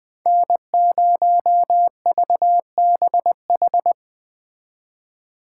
Morse code, heard as a keyed tone, about 20 wpm, 700 Hz. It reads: N0VBH
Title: Speed Racing - USA 1x3 Call Signs 17wpm